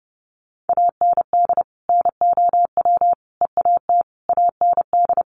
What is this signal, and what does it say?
Morse code, heard as a keyed tone, about 30 wpm, 700 Hz. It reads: AND NOW EAT AND
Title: Sentences from Top 1500 Words 30wpm